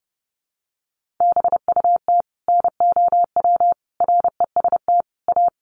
Morse code, heard as a keyed tone, about 30 wpm, 700 Hz. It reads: BUT NOW REST A
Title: Sentences from Top 1100 Words 30wpm